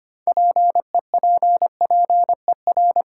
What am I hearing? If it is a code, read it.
Morse code, heard as a keyed tone, about 25 wpm, 700 Hz. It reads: PEPPER
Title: Sets of 5 Words - New Word Drill - 25wpm